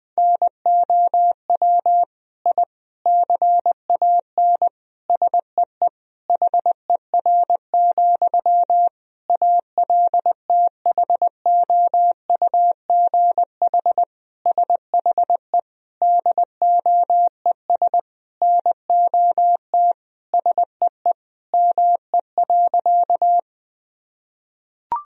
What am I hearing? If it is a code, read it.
Morse code, heard as a keyed tone, about 20 wpm, 700 Hz. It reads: NOW I CAN SEE HER, ALTHOUGH SHE DOES NOT SEE ME.
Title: Sentences from Top 1000 Words - Review Part 3 - 20wpm